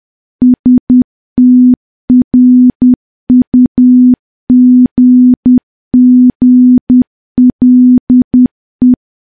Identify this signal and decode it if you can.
Morse code, heard as a keyed tone, about 10 wpm, 250 Hz. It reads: STRUGGLE
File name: morse / morse-training-10WPM